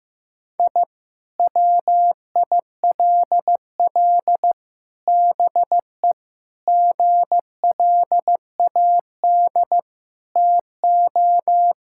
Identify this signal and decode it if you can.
Morse code, heard as a keyed tone, about 15 wpm, 700 Hz. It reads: I WILL BE GLAD TO
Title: Sentences from Top 1000 Words - Review Part 6 - 15wpm